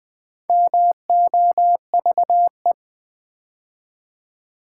Morse code, heard as a keyed tone, about 20 wpm, 700 Hz. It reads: MOVE
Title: Build A Word - 4 Letters At A Time - 8-Letter Words - 20wpm